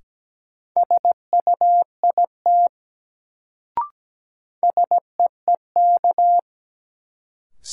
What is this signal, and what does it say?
Morse code, heard as a keyed tone, about 17 wpm, 700 Hz. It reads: SUIT SEEK
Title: Sentences from Top 1600 Words - New Word Drill - 17wpm